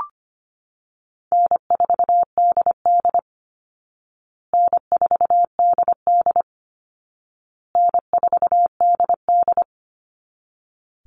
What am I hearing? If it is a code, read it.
Morse code, heard as a keyed tone, about 25 wpm, 700 Hz. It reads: N4DD N4DD N4DD
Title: Easy Call Signs Repeated 3x - 25wpm